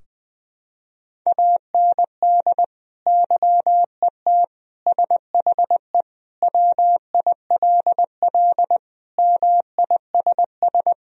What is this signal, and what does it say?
Morse code, heard as a keyed tone, about 20 wpm, 700 Hz. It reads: AND YET SHE WILL MISS
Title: Sentences from Top 1000 Words - Review Part 3 - 20wpm